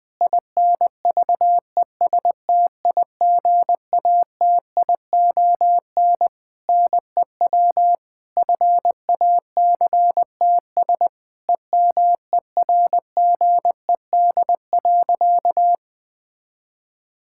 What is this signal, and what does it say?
Morse code, heard as a keyed tone, about 20 wpm, 700 Hz. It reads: INVESTIGATION NEW FACTS EMERGED.
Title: Sentences with 5-Word Phrases - 20wpm